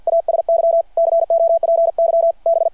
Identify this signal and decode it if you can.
Morse code, heard as a keyed tone, about 53 wpm, 690 Hz. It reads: ASY COPY B
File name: archived / CRSnet